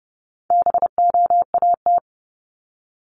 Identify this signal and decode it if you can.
Morse code, heard as a keyed tone, about 30 wpm, 700 Hz. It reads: BOAT
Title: Top 500 Words 30wpm